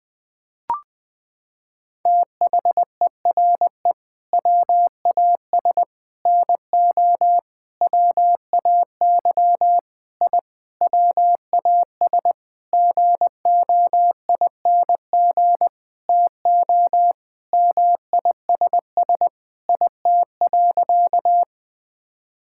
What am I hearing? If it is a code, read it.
Morse code, heard as a keyed tone, about 20 wpm, 700 Hz. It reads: THERE WAS NO WAY I WAS GOING TO MISS IT.
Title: Sentences with 7-Word Phrases - 20wpm